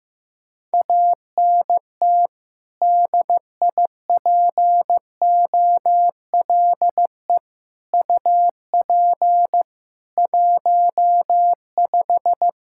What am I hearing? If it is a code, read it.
Morse code, heard as a keyed tone, about 16 wpm, 700 Hz. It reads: ANT DIPOLE UP 15
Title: Practice QSO - Exchange 2 - Info - 15wpm